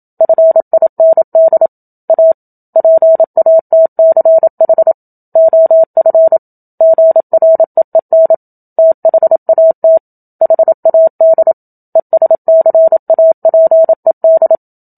Morse code, heard as a keyed tone, about 27 wpm, 650 Hz. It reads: FIND A PATCH OF GREEN THAT HAD ESCAPED
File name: Book Two / 01-Chapter 1 - 27 WPM